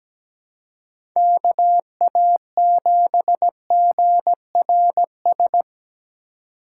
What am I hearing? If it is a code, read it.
Morse code, heard as a keyed tone, about 17 wpm, 700 Hz. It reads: KA7GRS
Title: USA Call Signs - 6 Characters - 2x Character Spacing - 17wpm